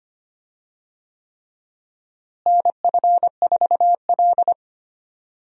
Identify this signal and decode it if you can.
Morse code, heard as a keyed tone, about 25 wpm, 700 Hz. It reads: NF4L
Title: USA Call Signs - 4 Characters - Repeated 3x - 25wpm